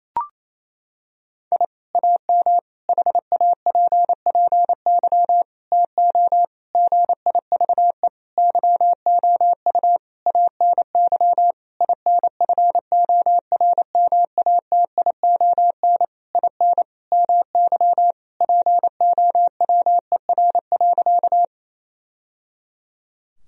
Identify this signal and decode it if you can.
Morse code, heard as a keyed tone, about 28 wpm, 700 Hz. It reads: I AM HAPPY TO GIVE YOU ANY INFORMATION IN MY POWER.
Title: Sentences from Top 1000 Words - Review Part 3 - 28wpm